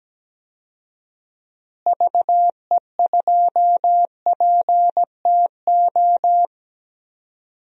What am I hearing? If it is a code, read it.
Morse code, heard as a keyed tone, about 17 wpm, 700 Hz. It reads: VE2PTO
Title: DXCC-100 Call Signs - 6 Characters - Repeated 3x - 17wpm